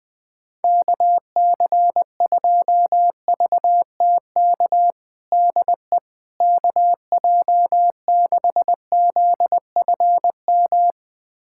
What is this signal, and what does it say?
Morse code, heard as a keyed tone, about 21 wpm, 700 Hz. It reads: KC2VTK DE KJ6ZFM
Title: USA Callsign DE Callsign 20wpm